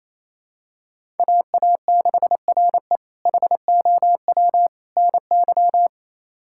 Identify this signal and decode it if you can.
Morse code, heard as a keyed tone, about 27 wpm, 700 Hz. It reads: AA6RE HOW NY
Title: K1USN SST QSO Practice - 28wpm